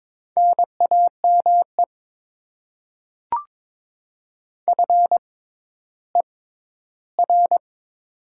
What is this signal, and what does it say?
Morse code, heard as a keyed tone, about 22 wpm, spaced as for 13 wpm, 700 Hz. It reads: NAME FER
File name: Top 20 QSO Elements - 6x Character Spacing - 22wpm